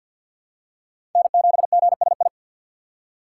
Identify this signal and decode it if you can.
Morse code, heard as a keyed tone, about 51 wpm, 700 Hz. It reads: N7GSI
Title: USA Call Signs - 5 Characters - 2x Character Spacing - 50wpm